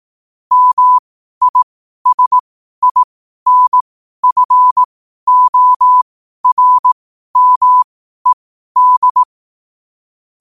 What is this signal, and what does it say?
Morse code, heard as a keyed tone, about 18 wpm, spaced as for 12 wpm, 1000 Hz. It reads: MISINFORMED
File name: qotdfiles / 12wpm